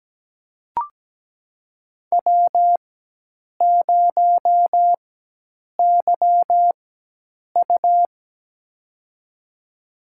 Morse code, heard as a keyed tone, about 17 wpm, spaced as for 8 wpm, 700 Hz. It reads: W0YU
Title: USA Call Signs - 4 Characters - 4x Character Spacing - 17wpm